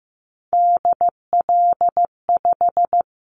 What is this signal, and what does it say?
Morse code, heard as a keyed tone, about 15 wpm, 700 Hz. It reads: DL5
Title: Callsign Prefixes for Top 50 DXCC - 15wpm